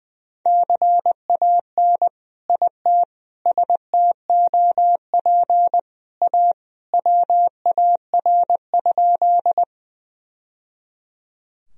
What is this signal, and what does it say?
Morse code, heard as a keyed tone, about 20 wpm, 700 Hz. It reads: CAN IT STOP A WAR?
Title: Sentences from Top 1000 Words - Review Part 2 - 20wpm